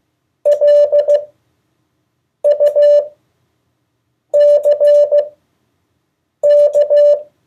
Morse code, heard as a keyed tone, about 15 wpm, spaced as for 6 wpm, 580 Hz. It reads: LUCK
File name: 2of4-words-1